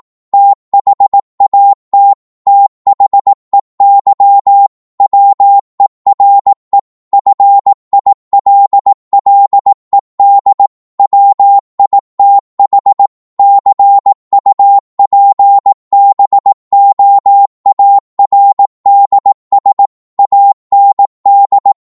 Morse code, heard as a keyed tone, about 18 wpm, 800 Hz. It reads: THAT THEY WERE FILLED WITH CUPBOARDS AND
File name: Morse / psg-18wpm